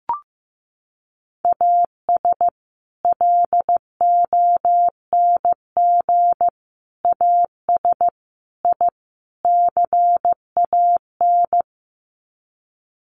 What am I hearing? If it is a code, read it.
Morse code, heard as a keyed tone, about 15 wpm, 700 Hz. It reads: AS LONG AS I CAN
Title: Sets of 5 Words Encore2 15wpm